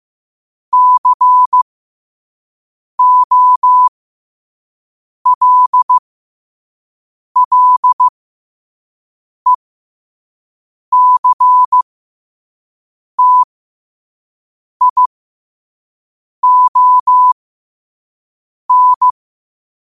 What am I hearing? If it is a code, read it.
Morse code, heard as a keyed tone, about 15 wpm, spaced as for 5 wpm, 1000 Hz. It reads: COLLECTION
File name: qotdfiles / 5wpm